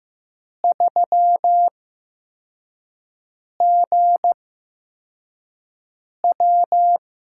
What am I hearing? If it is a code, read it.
Morse code, heard as a keyed tone, about 15 wpm, spaced as for 4 wpm, 700 Hz. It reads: 3GW
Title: USA Call Signs - 5 Characters - 8x Character Spacing - 15wpm